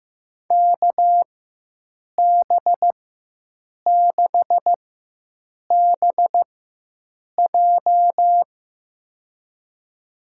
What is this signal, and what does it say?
Morse code, heard as a keyed tone, about 15 wpm, spaced as for 7 wpm, 700 Hz. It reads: KB6BJ
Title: USA Call Signs - 5 Characters - 4x Character Spacing - 15wpm